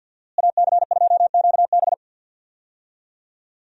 Morse code, heard as a keyed tone, about 52 wpm, 700 Hz. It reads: AC2QB
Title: USA Call Signs - 5 Characters - Repeated 2x - 50wpm